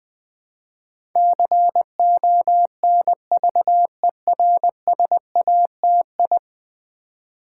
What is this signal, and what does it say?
Morse code, heard as a keyed tone, about 20 wpm, 700 Hz. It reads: CONVERSATI
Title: Build A Word - 5 Letters At A Time - 12-Letter Words - 20wpm